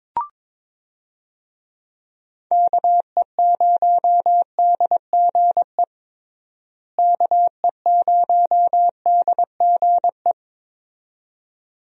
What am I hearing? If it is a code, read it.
Morse code, heard as a keyed tone, about 22 wpm, 700 Hz. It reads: KE0DGE KE0DGE
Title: USA Call Signs - 6 Characters - Repeated 2x - 22wpm